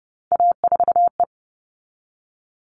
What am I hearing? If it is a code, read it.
Morse code, heard as a keyed tone, about 30 wpm, 700 Hz. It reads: A4E
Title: All Cut Numbers - 3-Digit 30wpm